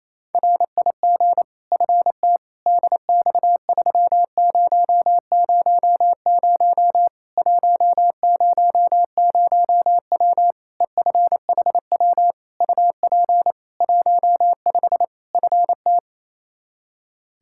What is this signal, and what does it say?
Morse code, heard as a keyed tone, about 28 wpm, 700 Hz. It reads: RIG FT DX3000 100W EFHW UP 15 FT
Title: Practice QSO - Exchange 2 - Info - 28wpm